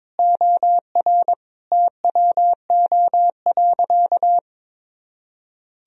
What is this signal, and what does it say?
Morse code, heard as a keyed tone, about 22 wpm, 700 Hz. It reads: OR TWO.
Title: Sentences from Top 1000 Words - Review Part 1 - 22wpm